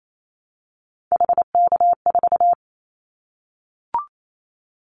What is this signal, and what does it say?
Morse code, heard as a keyed tone, about 28 wpm, 700 Hz. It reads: HK4
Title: Callsign Prefixes for Top 100 DXCC - 28wpm